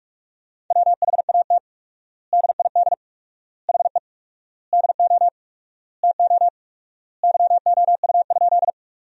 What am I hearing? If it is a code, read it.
Morse code, heard as a keyed tone, about 46 wpm, 700 Hz. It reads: WHAT DID HE DO TO YOU?
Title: 4x Word-Spacing - Sentences from Top 100 Words - 45wpm